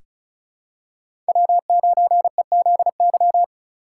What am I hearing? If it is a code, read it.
Morse code, heard as a keyed tone, about 36 wpm, 700 Hz. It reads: W9EZY
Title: USA Call Signs - 5 Characters - 4x Character Spacing - 35wpm